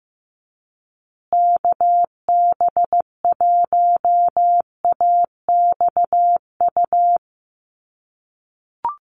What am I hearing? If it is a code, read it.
Morse code, heard as a keyed tone, about 15 wpm, 700 Hz. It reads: KB1AXU
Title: USA Call Signs Repeated 3x 15wpm